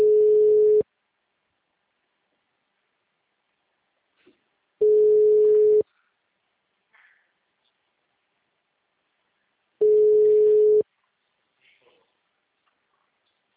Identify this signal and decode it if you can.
Ringback tone (Europe, 425 Hz).